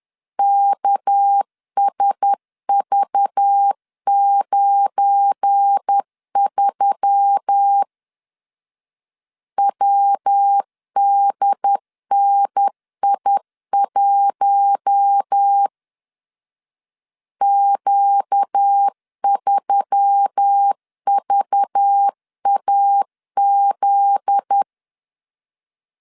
Morse code, heard as a keyed tone, about 10 wpm, 800 Hz. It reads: KSV93 WDNI1 Q3VAZ
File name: LEZIONI / 7 GF9Z / 7 GF9Z 10wpm 2611